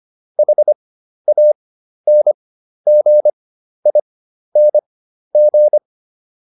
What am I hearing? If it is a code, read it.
Morse code, heard as a keyed tone, about 25 wpm, spaced as for 12 wpm, 600 Hz. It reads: HANGING